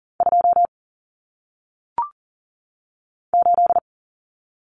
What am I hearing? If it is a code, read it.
Morse code, heard as a keyed tone, about 39 wpm, 700 Hz. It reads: 2 8
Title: Common Cut Numbers - 1-Digit 40wpm